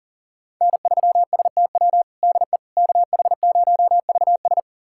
Morse code, heard as a keyed tone, about 37 wpm, 700 Hz. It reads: N3STW DE KH0VS
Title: USA Callsign DE Callsign - 5 Characters - 40wpm